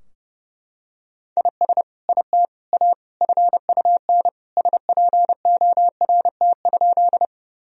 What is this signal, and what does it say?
Morse code, heard as a keyed tone, about 32 wpm, 700 Hz. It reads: IS IT A FUN SPORT?
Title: Sentences from Top 1000 Words - Review Part 4 - 30wpm